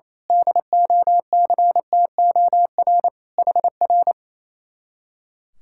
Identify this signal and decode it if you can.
Morse code, heard as a keyed tone, about 27 wpm, 700 Hz. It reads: DOCTOR HR
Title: Speed-Racing Common QSO Phrases - 28wpm